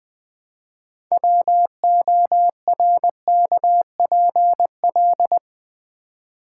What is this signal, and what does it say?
Morse code, heard as a keyed tone, about 20 wpm, 700 Hz. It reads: WORKPL
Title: Build A Word - 3 Letters At A Time - 9-Letter Words - 20wpm